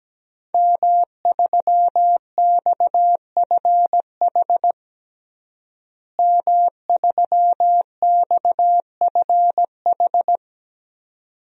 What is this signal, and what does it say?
Morse code, heard as a keyed tone, about 17 wpm, 700 Hz. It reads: M3XFH M3XFH
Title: DXCC-100 Call Signs - 5 Characters - Repeated 2x - 17wpm